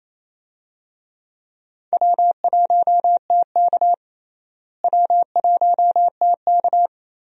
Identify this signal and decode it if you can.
Morse code, heard as a keyed tone, about 28 wpm, 700 Hz. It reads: W1TK W1TK
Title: USA Call Signs - 4 Characters - Repeated 2x - 28wpm